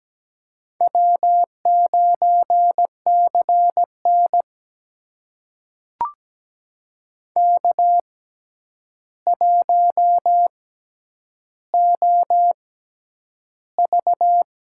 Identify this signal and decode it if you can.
Morse code, heard as a keyed tone, about 17 wpm, spaced as for 9 wpm, 700 Hz. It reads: W9CN K1OV
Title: USA Call Signs - 6x Character Spacing - 17wpm